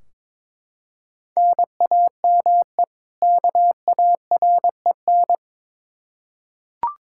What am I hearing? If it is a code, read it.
Morse code, heard as a keyed tone, about 22 wpm, 700 Hz. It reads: NAME KAREN
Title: Practice QSO - Exchange 1 - Beginning - 22wpm